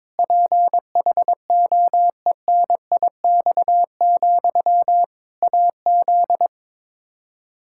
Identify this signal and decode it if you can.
Morse code, heard as a keyed tone, about 22 wpm, 700 Hz. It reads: PHOENIX, AZ
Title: Practice QSO - Exchange 1 - Beginning - 22wpm